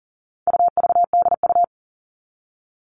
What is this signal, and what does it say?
Morse code, heard as a keyed tone, about 40 wpm, 700 Hz. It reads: UVDU
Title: All Cut Numbers - 4-Digit 40wpm